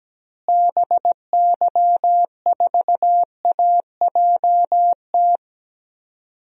Morse code, heard as a keyed tone, about 17 wpm, 700 Hz. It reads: BY4AJT
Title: DXCC-100 Call Signs - 6 Characters - Repeated 2x - 17wpm